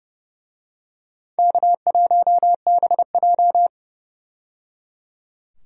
Morse code, heard as a keyed tone, about 30 wpm, 700 Hz. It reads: K1BJ
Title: USA Call Signs - 4 Characters - 30wpm